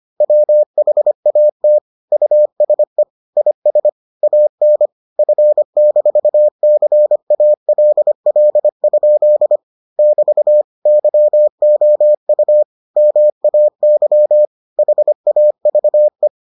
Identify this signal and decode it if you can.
Morse code, heard as a keyed tone, about 25 wpm, 600 Hz. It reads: WHAT USE IS AN F-CALL? <BT> YOU MAY HAVE